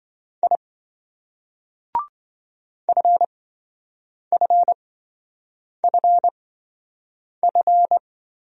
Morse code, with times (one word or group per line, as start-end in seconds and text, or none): 0.4-0.6 I
2.9-8.0 FFFF